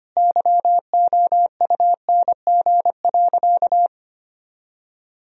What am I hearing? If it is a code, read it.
Morse code, heard as a keyed tone, about 25 wpm, 700 Hz. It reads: YOUNG.
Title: Sentences from Top 1000 Words - Review Part 1 - 25wpm